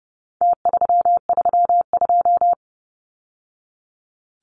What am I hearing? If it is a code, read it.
Morse code, heard as a keyed tone, about 30 wpm, 700 Hz. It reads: T332
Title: Numbers - Contest Style - 4-Digit 30wpm